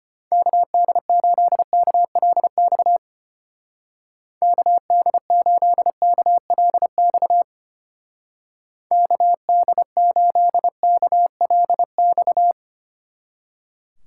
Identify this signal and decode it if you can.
Morse code, heard as a keyed tone, about 32 wpm, 700 Hz. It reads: KD8KLX KD8KLX KD8KLX